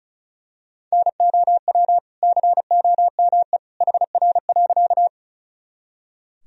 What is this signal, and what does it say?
Morse code, heard as a keyed tone, about 35 wpm, 700 Hz. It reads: NOW COME HR.
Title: Sentences from Top 200 Words and Abbreviations 35wpm